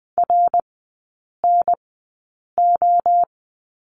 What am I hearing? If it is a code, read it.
Morse code, heard as a keyed tone, about 20 wpm, spaced as for 8 wpm, 700 Hz. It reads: RNO
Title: 053 Lesson 10e - Letter r - Three Characters 20wpm